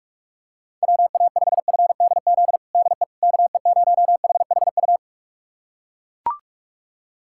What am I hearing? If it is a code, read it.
Morse code, heard as a keyed tone, about 41 wpm, 700 Hz. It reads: WA5FDZ DE KE0HHU
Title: USA Callsign DE Callsign 45wpm